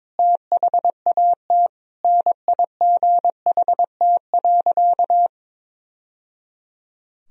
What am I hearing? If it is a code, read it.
Morse code, heard as a keyed tone, about 22 wpm, 700 Hz. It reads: THAT NIGHT.
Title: Sentences from Top 1000 Words - Review Part 2 - 22wpm